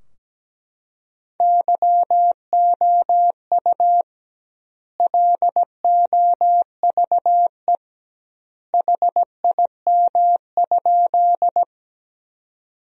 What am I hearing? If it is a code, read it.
Morse code, heard as a keyed tone, about 17 wpm, 700 Hz. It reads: YOU LOVE HIM?
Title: 2x Word-Spacing - Sentences from Top 400 Words - 17wpm